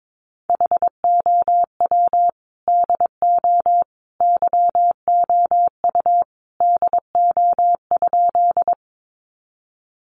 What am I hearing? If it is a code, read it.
Morse code, heard as a keyed tone, about 22 wpm, 700 Hz. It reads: HOW DO YOU DO?
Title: Sentences from Top 100 Words 22wpm